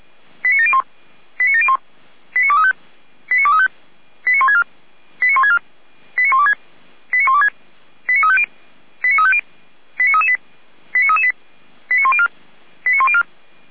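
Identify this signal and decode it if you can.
Five-tone selective call (ZVEI), heard as quick runs of five tones, tones of about 70 ms. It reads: 89701 89701 89235 89235 89164 89164 89126 89126 89350 89350 89338 89338 89104 89104
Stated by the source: Five-tone ZVEI selective call received on an SDR, channel or frequency 455.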